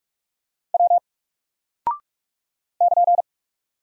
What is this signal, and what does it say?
Morse code, heard as a keyed tone, about 44 wpm, 700 Hz. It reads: W <KN>
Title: Single Character (Letter:Number:Punctuation:Prosign) 45wpm